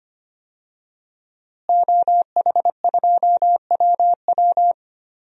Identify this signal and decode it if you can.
Morse code, heard as a keyed tone, about 25 wpm, 700 Hz. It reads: OH2WW
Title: DXCC-100 Call Signs - 5 Characters - Repeated 3x - 25wpm